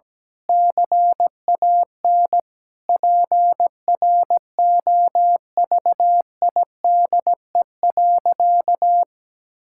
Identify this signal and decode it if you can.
Morse code, heard as a keyed tone, about 17 wpm, 700 Hz. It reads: CAN PROVIDE.
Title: Sentences from Top 1000 Words - Review Part 6 - 17wpm